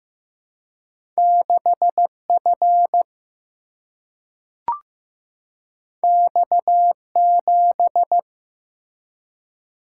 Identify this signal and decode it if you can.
Morse code, heard as a keyed tone, about 15 wpm, 700 Hz. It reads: 6F X7
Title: Single-Letter Single-Number 15wpm